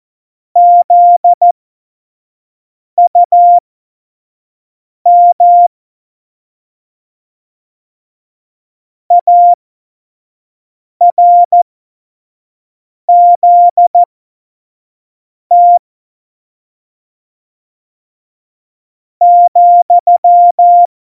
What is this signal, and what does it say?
Morse code, heard as a keyed tone, about 14 wpm, spaced as for 4 wpm, 700 Hz. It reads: ZUM ARZT,
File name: Morsekurs / Witze / Witz 06. Februar 2018, 25 F70